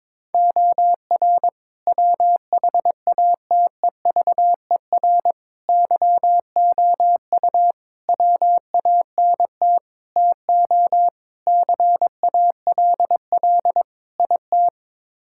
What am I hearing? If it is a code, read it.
Morse code, heard as a keyed tone, about 22 wpm, 700 Hz. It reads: OR WHATEVER YOU WANT TO CALL IT
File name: practice / 22wpm / Sets of 7 Words - 22wpm